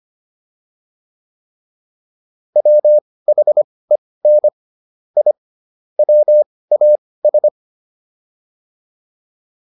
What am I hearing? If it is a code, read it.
Morse code, heard as a keyed tone, about 25 wpm, spaced as for 18 wpm, 600 Hz. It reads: WHEN I WAS